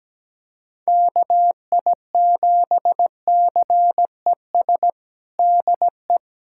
Morse code, heard as a keyed tone, about 17 wpm, 700 Hz. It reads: KI7CES DE
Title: USA Callsign DE Callsign 17wpm